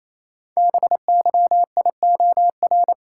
Morse code, heard as a keyed tone, about 27 wpm, 700 Hz. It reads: BYIOR
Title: Code Groups - 5 Letters - 28wpm